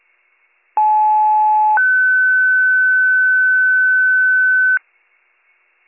Two-tone page, heard as a one-second tone, then a three-second tone, about 860 then 1500 Hz. It is heard from 0.8 to 4.8 s.